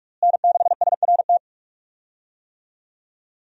Morse code, heard as a keyed tone, about 41 wpm, 700 Hz. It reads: N6SRT
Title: USA Call Signs - 5 Characters - Repeated 2x - 45wpm